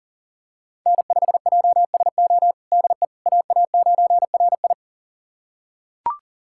Morse code, heard as a keyed tone, about 40 wpm, 700 Hz. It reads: N5JSO DE AA9RI
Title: USA Callsign DE Callsign - 5 Characters - 40wpm